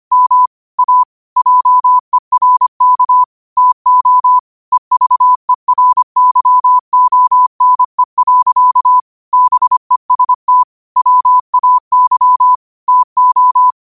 Morse code, heard as a keyed tone, about 25 wpm, 1000 Hz. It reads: M A JERK TO EVERYONE. BEST WAY TO
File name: qotdfiles / 25wpm